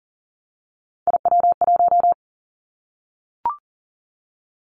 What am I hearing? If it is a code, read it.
Morse code, heard as a keyed tone, about 40 wpm, 700 Hz. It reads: IW1
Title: Callsign Prefixes for Top 25 DXCC - Repeated 3x - 40wpm